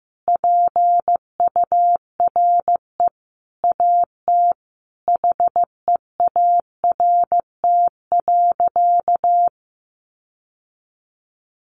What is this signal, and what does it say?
Morse code, heard as a keyed tone, about 15 wpm, 700 Hz. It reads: PURE AT HEART.
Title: Sentences from Top 1500 Words 15wpm